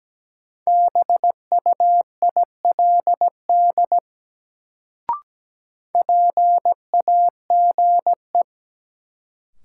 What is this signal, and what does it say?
Morse code, heard as a keyed tone, about 17 wpm, 700 Hz. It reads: BUILD PAGE
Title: Sentences from Top 1000 Words - Review Part 1 - New Word Drill - 17wpm